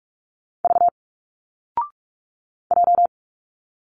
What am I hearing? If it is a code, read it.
Morse code, heard as a keyed tone, about 46 wpm, 700 Hz. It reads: V J
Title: Single Letters 45wpm